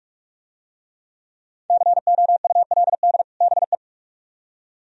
Morse code, heard as a keyed tone, about 43 wpm, 700 Hz. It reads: COULD BE